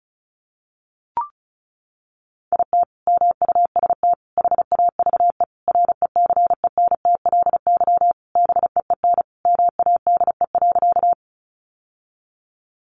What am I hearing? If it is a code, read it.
Morse code, heard as a keyed tone, about 34 wpm, 700 Hz. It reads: IT MUST HAVE RECENTLY BEEN MADE.
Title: Sentences from Top 1200 Words 35wpm